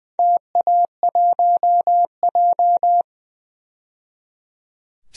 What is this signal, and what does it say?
Morse code, heard as a keyed tone, about 20 wpm, 700 Hz. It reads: TA1J
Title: DXCC-100 Call Signs - 4 Characters - 20wpm